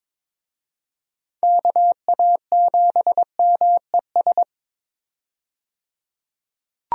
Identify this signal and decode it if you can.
Morse code, heard as a keyed tone, about 22 wpm, 700 Hz. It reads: KA7MES
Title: USA Call Signs - 6 Characters - 22wpm